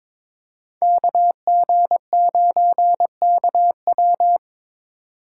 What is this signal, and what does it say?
Morse code, heard as a keyed tone, about 22 wpm, 700 Hz. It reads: KG9KW
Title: USA Call Signs - 5 Characters - 3x Character Spacing - 22wpm